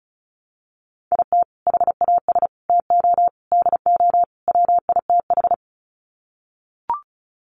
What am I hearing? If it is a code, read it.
Morse code, heard as a keyed tone, about 34 wpm, 700 Hz. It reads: IT HAS TO DO WITH
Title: Sets of 5 Words 35wpm